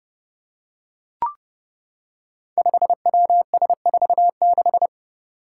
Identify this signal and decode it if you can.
Morse code, heard as a keyed tone, about 30 wpm, 700 Hz. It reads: 5WS46
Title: Five Challenging Letter-Number Combinations 30wpm